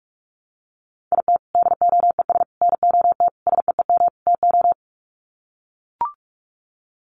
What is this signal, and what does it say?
Morse code, heard as a keyed tone, about 44 wpm, 700 Hz. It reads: IT DOES NOT SEEM TO
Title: Sets of 5 Words Encore2 45wpm